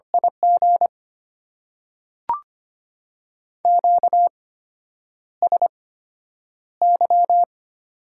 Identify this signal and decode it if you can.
Morse code, heard as a keyed tone, about 25 wpm, spaced as for 9 wpm, 700 Hz. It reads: IG QSY
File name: Top 90 QSO Elements - 8x Character Spacing - 25wpm